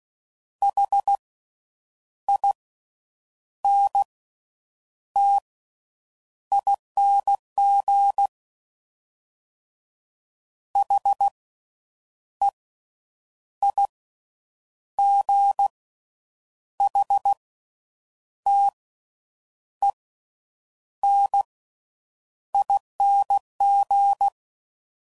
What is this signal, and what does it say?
Morse code, heard as a keyed tone, about 16 wpm, spaced as for 7 wpm, 800 Hz. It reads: HINTING HEIGHTENING
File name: Slow With Faster Ings